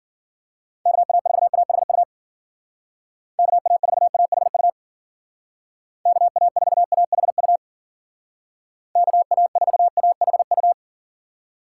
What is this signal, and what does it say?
Morse code, heard as a keyed tone, about 53 wpm, 700 Hz. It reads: KA4AHU KA4AHU KA4AHU KA4AHU